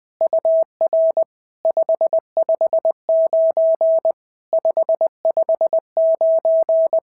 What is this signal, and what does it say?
Morse code, heard as a keyed tone, about 20 wpm, 690 Hz. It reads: UR 559 559
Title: SOTA QSO Practice 20wpm